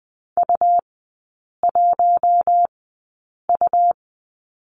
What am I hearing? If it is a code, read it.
Morse code, heard as a keyed tone, about 20 wpm, spaced as for 8 wpm, 700 Hz. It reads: U1U
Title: 100 Lesson 17e - Letter u - Three Characters 20wpm